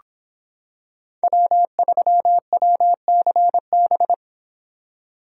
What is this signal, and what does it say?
Morse code, heard as a keyed tone, about 25 wpm, 700 Hz. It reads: W3WCB